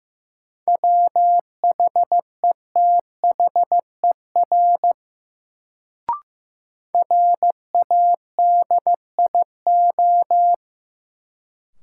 Morse code, heard as a keyed tone, about 15 wpm, 700 Hz. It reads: WHETHER RADIO
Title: Sentences from Top 1000 Words - Review Part 3 - New Word Drill - 15wpm